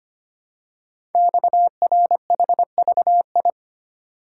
Morse code, heard as a keyed tone, about 25 wpm, 700 Hz. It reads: XRHVI
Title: Code Groups - 5 Letters - 25wpm